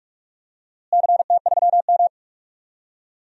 Morse code, heard as a keyed tone, about 46 wpm, 700 Hz. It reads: CT3M
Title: DXCC-100 Call Signs - 4 Characters - Repeated 2x - 45wpm